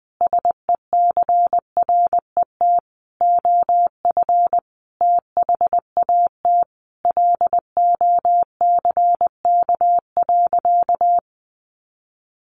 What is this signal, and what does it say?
Morse code, heard as a keyed tone, about 20 wpm, 700 Hz. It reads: SECRET OF THAT LOCK.
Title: Sentences from Top 1300 Words 20wpm